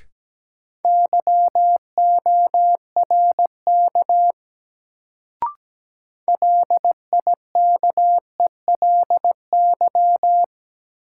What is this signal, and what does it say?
Morse code, heard as a keyed tone, about 17 wpm, 700 Hz. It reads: YORK LIKELY
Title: Sets of 2 Words - New Word Drill - 17wpm